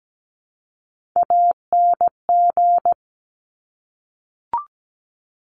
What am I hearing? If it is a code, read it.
Morse code, heard as a keyed tone, about 17 wpm, 700 Hz. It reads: ANG
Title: Top Three Letter Combinations in Words 17wpm